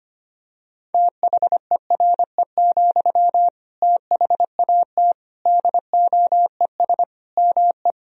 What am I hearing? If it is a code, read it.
Morse code, heard as a keyed tone, about 25 wpm, 700 Hz. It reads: THERE, THAT DOES ME
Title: Sentences from Top 200 Words and Abbreviations 25wpm